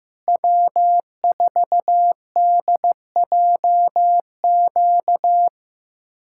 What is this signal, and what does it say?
Morse code, heard as a keyed tone, about 15 wpm, 700 Hz. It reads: W4DJQ
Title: USA Callsign DE Callsign - 5 Characters - 15wpm